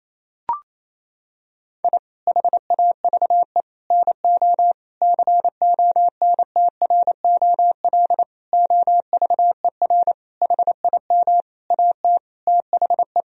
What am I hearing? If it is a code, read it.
Morse code, heard as a keyed tone, about 28 wpm, 700 Hz. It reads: I HAVE NO CONTROL OVER HIM AT THE
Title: Sentences from Top 1000 Words - Review Part 3 - 28wpm